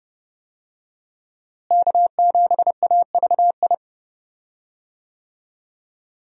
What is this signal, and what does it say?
Morse code, heard as a keyed tone, about 30 wpm, 700 Hz. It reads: K7AVI
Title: USA Call Signs - 5 Characters - Repeated 2x - 30wpm